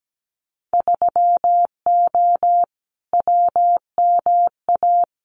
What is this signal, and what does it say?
Morse code, heard as a keyed tone, about 17 wpm, 700 Hz. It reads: 3O WMA
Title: Winter Field Day Exchanges Encore 17wpm